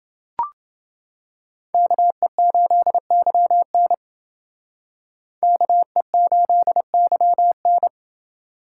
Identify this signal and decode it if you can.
Morse code, heard as a keyed tone, about 29 wpm, 700 Hz. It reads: KE8YN KE8YN